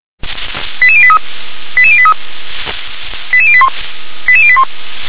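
Five-tone selective call (ZVEI), heard as quick runs of five tones, tones of about 70 ms. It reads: 88083 88083 88081 88081
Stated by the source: ZVEI five-tone call 88148 received on SDR.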